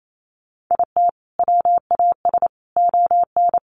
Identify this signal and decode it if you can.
Morse code, heard as a keyed tone, about 28 wpm, 700 Hz. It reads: IT WAS ON
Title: Sentences from Top 800 Words 28wpm